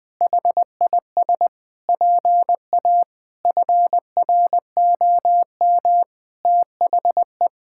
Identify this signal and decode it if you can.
Morse code, heard as a keyed tone, about 21 wpm, 700 Hz. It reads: HIS PA FROM THE
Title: Pangram Challenge v2 - 20wpm